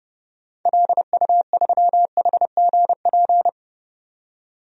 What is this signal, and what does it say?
Morse code, heard as a keyed tone, about 32 wpm, 700 Hz. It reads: LU3HGP
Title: DXCC-100 Call Signs - 6 Characters - 30wpm